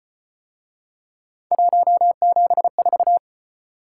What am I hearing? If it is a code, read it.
Morse code, heard as a keyed tone, about 34 wpm, 700 Hz. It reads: 174